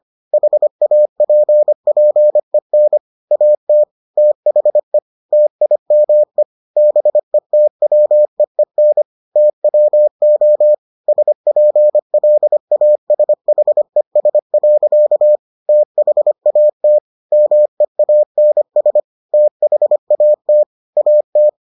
Morse code, heard as a keyed tone, about 25 wpm, 600 Hz. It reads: HAPPEN AT THE TIME BETWEEN TWO SPLASHES. THAT MEANS THAT AT